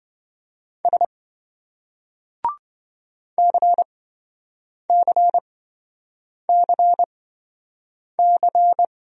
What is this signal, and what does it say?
Morse code, heard as a keyed tone, about 27 wpm, spaced as for 7 wpm, 700 Hz. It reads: S CCCC